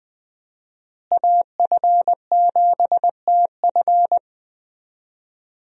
Morse code, heard as a keyed tone, about 20 wpm, 700 Hz. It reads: AF7TF
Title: USA Call Signs - 5 Characters - Repeated 2x - 20wpm